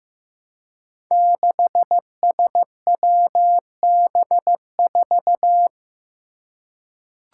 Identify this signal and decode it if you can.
Morse code, heard as a keyed tone, about 15 wpm, 700 Hz. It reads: 6SWB4
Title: Five Challenging Letter-Number Combinations 15wpm